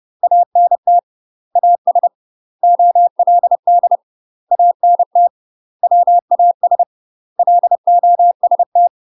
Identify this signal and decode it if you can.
Morse code, heard as a keyed tone, about 30 wpm, 700 Hz. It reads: ANT AS OLD ANT WAS LOST
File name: UK MASTER QSOs TRAINING / 30WPM / UK MASTER QSOs 0004